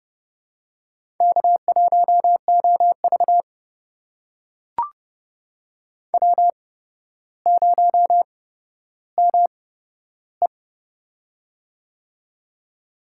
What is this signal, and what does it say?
Morse code, heard as a keyed tone, about 30 wpm, spaced as for 13 wpm, 700 Hz. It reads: K1OV W0ME
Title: USA Call Signs - 4 Characters - 8x Character Spacing - 30wpm